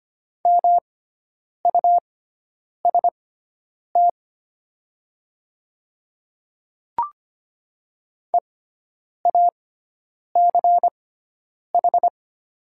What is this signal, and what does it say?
Morse code, heard as a keyed tone, about 25 wpm, spaced as for 9 wpm, 700 Hz. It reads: MUST EACH
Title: 6x Character Spacing - Top 300 Words - 25wpm